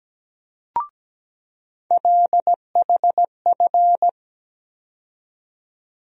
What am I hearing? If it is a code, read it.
Morse code, heard as a keyed tone, about 17 wpm, 700 Hz. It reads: LHF
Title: USA Call Sign Postfixes - 3 Letters 17wpm